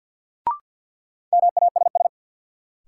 Morse code, heard as a keyed tone, about 53 wpm, 700 Hz. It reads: MASS
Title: Sets of 5 Words Encore - New Word Drill - 50wpm